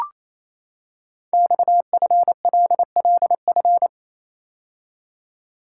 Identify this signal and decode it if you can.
Morse code, heard as a keyed tone, about 29 wpm, 700 Hz. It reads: XFLLF
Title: Code Groups - 5 Letters - 28wpm